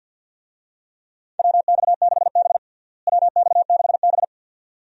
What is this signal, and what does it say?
Morse code, heard as a keyed tone, about 52 wpm, 700 Hz. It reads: WX6B WX6B
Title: USA Call Signs - 4 Characters - Repeated 2x - 50wpm